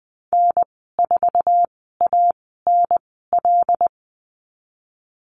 Morse code, heard as a keyed tone, about 20 wpm, spaced as for 15 wpm, 700 Hz. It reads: N4ANL
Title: 186 Lesson 30g - Letter b - Call Signs 20wpm